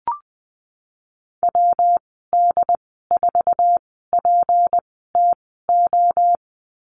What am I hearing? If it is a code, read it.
Morse code, heard as a keyed tone, about 20 wpm, 700 Hz. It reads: W D 4 P T O
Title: 154 Lesson 25g - Letter p - Call Signs 20wpm